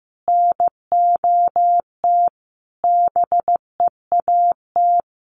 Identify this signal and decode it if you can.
Morse code, heard as a keyed tone, about 14 wpm, 700 Hz. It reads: NOT BEAT
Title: Sentences from Top 1300 Words 15wpm